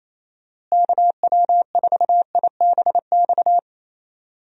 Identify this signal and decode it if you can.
Morse code, heard as a keyed tone, about 28 wpm, 700 Hz. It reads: KW4IBX
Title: USA Call Signs - 6 Characters - Repeated 3x - 28wpm